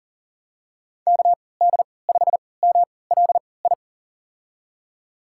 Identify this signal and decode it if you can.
Morse code, heard as a keyed tone, about 40 wpm, spaced as for 23 wpm, 700 Hz. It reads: KD5MLI
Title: USA Call Signs - 6 Characters - 3x Character Spacing - 40wpm